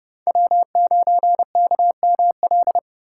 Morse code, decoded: W9KML

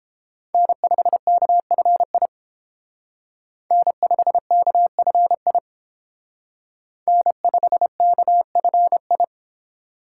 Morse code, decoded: N5KFI N5KFI N5KFI

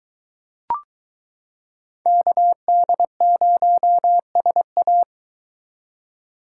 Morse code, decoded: KD0SA

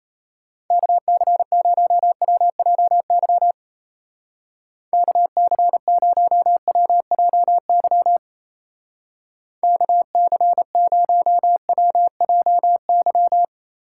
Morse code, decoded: KC0WJY KC0WJY KC0WJY

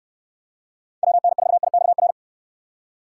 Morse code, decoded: WN4IXR

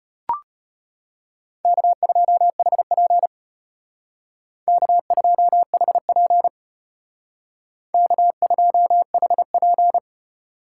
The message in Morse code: K2HP K2HP K2HP

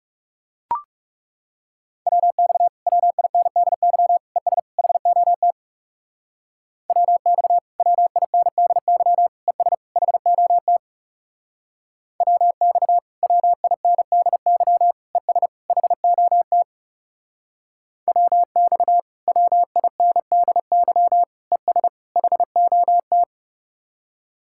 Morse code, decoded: WX WINDY ES HOT WX WINDY ES HOT WX WINDY ES HOT WX WINDY ES HOT

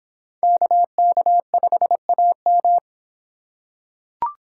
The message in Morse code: KK5AM

